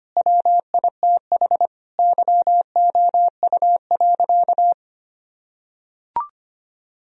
Morse code, heard as a keyed tone, about 25 wpm, 700 Hz. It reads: WITH YOU.
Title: Sentences from Top 1000 Words - Review Part 4 - 25wpm